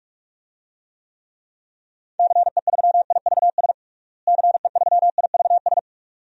Morse code, heard as a keyed tone, about 47 wpm, 690 Hz. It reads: CE3IVS CE3IVS
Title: DXCC-100 Call Signs - 6 Characters - Repeated 2x - 45wpm